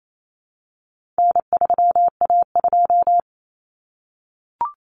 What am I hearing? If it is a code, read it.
Morse code, heard as a keyed tone, about 28 wpm, 700 Hz. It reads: N3A2
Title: Common Cut Numbers - 4-Digit 28wpm